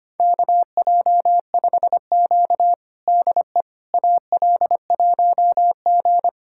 Morse code, heard as a keyed tone, about 25 wpm, 700 Hz. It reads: KJ5Q DE AL1G
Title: USA Callsign DE Callsign - 4 Characters - 25wpm